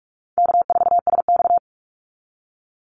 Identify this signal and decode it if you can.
Morse code, heard as a keyed tone, about 46 wpm, 700 Hz. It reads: K4SX
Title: Easy Call Signs - 45wpm